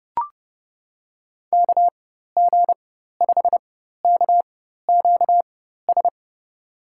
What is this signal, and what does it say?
Morse code, heard as a keyed tone, about 30 wpm, spaced as for 14 wpm, 700 Hz. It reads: KG5KQS
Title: USA Call Signs - 6 Characters - 4x Character Spacing - 30wpm